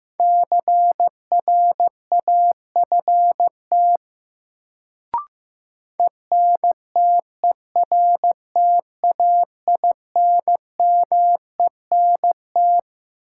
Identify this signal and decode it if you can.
Morse code, heard as a keyed tone, about 15 wpm, 700 Hz. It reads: CRAFT ENTERTAINMENT
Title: Sentences from Top 1000 Words - Review Part 6 - New Word Drill - 15wpm